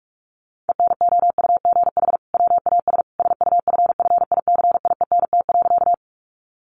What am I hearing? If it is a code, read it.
Morse code, heard as a keyed tone, about 44 wpm, 700 Hz. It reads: ENOUGH WAS SUFFICIENT.
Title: Sentences from Top 1400 Words 45wpm